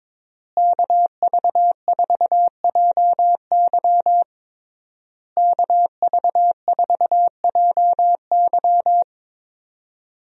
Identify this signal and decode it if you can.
Morse code, heard as a keyed tone, about 22 wpm, 700 Hz. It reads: KV4JY KV4JY
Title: USA Call Signs - 5 Characters - Repeated 3x - 22wpm